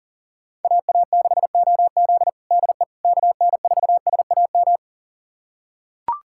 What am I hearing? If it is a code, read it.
Morse code, heard as a keyed tone, about 40 wpm, 700 Hz. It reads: AA6OZ DE KN4SAM
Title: USA Callsign DE Callsign 40wpm